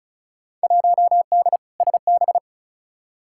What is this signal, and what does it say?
Morse code, heard as a keyed tone, about 36 wpm, 700 Hz. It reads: 1D SB
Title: Field Day Exchanges - 35wpm